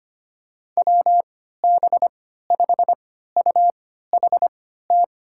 Morse code, heard as a keyed tone, about 25 wpm, spaced as for 14 wpm, 700 Hz. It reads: WB5UHT
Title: USA Call Signs - 6 Characters - 3x Character Spacing - 25wpm